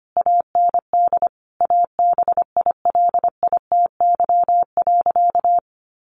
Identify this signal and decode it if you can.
Morse code, heard as a keyed tone, about 25 wpm, 700 Hz. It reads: AND ABILITY.
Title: Sentences from Top 1200 Words 25wpm